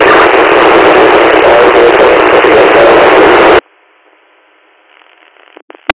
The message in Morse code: none

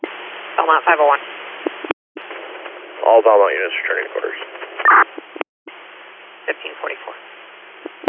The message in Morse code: none

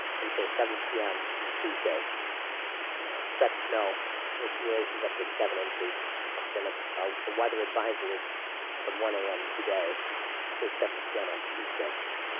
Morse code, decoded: none

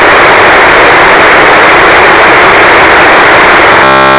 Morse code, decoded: none

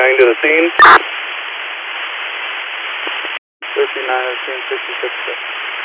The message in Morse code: none

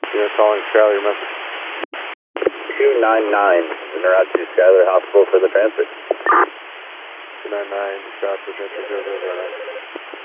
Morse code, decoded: none

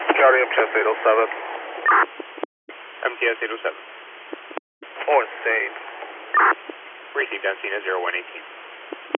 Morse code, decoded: none

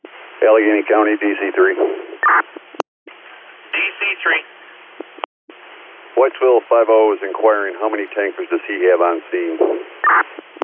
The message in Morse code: none

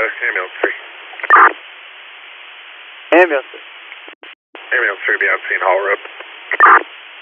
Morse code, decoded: none

